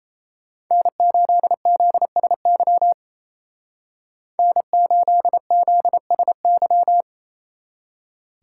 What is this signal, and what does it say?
Morse code, heard as a keyed tone, about 33 wpm, 700 Hz. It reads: N8ZSY N8ZSY
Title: Speed Racing - USA 1x3 Call Signs 28wpm